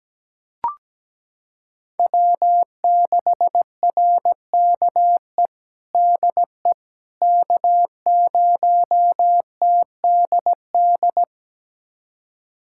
Morse code, heard as a keyed tone, about 17 wpm, 700 Hz. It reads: W6RKE DE K0TDD
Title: USA Callsign DE Callsign - 5 Characters - 17wpm